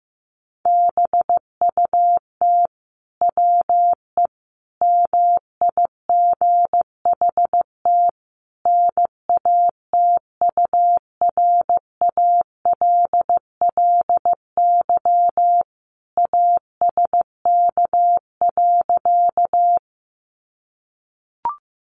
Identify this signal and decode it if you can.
Morse code, heard as a keyed tone, about 15 wpm, 700 Hz. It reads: BUT WE MIGHT NATURALLY ASK.
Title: Sentences from Top 1300 Words 15wpm